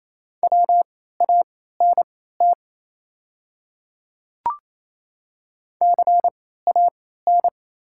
Morse code, heard as a keyed tone, about 28 wpm, spaced as for 16 wpm, 700 Hz. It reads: WANT CAN
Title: 3x Character Spacing - Top 200 Words - 28wpm